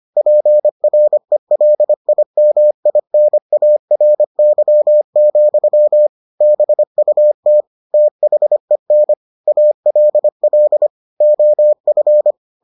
Morse code, heard as a keyed tone, about 25 wpm, 600 Hz. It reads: PRELIMINARY, BUT THEN ALL OF